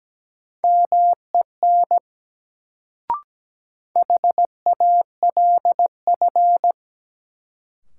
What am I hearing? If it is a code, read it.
Morse code, heard as a keyed tone, about 17 wpm, 700 Hz. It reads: MEN HALF
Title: Sets of 3 Words - New Word Drill - 17wpm